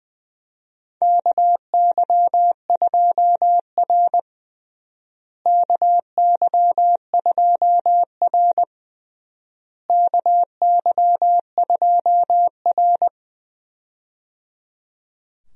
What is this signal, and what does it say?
Morse code, heard as a keyed tone, about 19 wpm, 700 Hz. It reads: KY2R KY2R KY2R
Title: USA Call Signs - 4 Characters - Repeated 3x - 20wpm